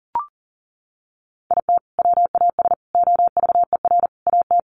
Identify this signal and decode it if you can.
Morse code, heard as a keyed tone, about 40 wpm, 700 Hz. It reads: IT WAS OVER AT
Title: Sentences from Top 600 Words 40wpm